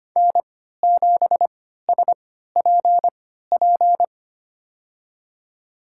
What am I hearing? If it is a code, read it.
Morse code, heard as a keyed tone, about 25 wpm, spaced as for 14 wpm, 700 Hz. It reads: N7SPP
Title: USA Call Signs - 5 Characters - 3x Character Spacing - 25wpm